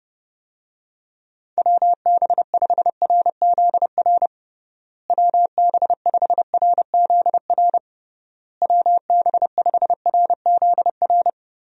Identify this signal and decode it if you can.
Morse code, heard as a keyed tone, about 32 wpm, 700 Hz. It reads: WB5RZR WB5RZR WB5RZR
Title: USA Call Signs - 6 Characters - Repeated 3x - 30wpm